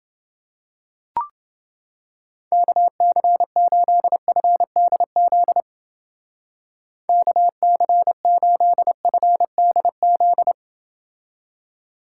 Morse code, heard as a keyed tone, about 31 wpm, 700 Hz. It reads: KC8FDZ KC8FDZ